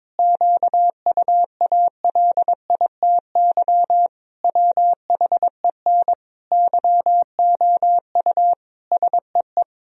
Morse code, heard as a keyed tone, about 22 wpm, 700 Hz. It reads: QUALITY WHEN YOU SEE
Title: Sentences from Top 1000 Words - Review Part 4 - 22wpm